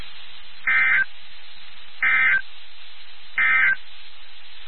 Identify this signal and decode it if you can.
Emergency Alert System (SAME) tones, heard as harsh buzzing bursts. It is heard from 0.5 to 3.8 s.
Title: NOAA Weather Radio WXK79